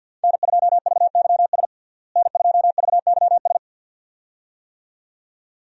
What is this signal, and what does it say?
Morse code, heard as a keyed tone, about 52 wpm, 700 Hz. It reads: N2VYS N2VYS
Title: USA Call Signs - 5 Characters - Repeated 2x - 50wpm